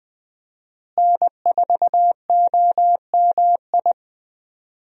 Morse code, heard as a keyed tone, about 20 wpm, 700 Hz. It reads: N4OMI
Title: USA Call Signs - 5 Characters - Repeated 2x - 20wpm